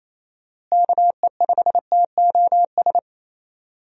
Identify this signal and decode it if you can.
Morse code, heard as a keyed tone, about 28 wpm, 700 Hz. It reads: KE5TOS
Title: USA Call Signs - 6 Characters - Repeated 2x - 28wpm